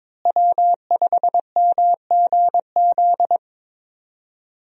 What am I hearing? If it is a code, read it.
Morse code, heard as a keyed tone, about 22 wpm, 700 Hz. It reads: W5MGZ
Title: USA Call Signs - 5 Characters - Repeated 2x - 22wpm